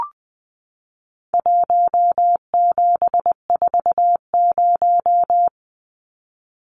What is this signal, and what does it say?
Morse code, heard as a keyed tone, about 20 wpm, 700 Hz. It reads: 1740
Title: Numbers - 4-Digit 20wpm